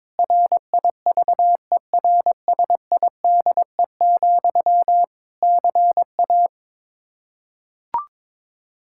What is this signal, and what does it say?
Morse code, heard as a keyed tone, about 22 wpm, 700 Hz. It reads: RIVERSIDE, CA
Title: Practice QSO - Exchange 1 - Beginning - 22wpm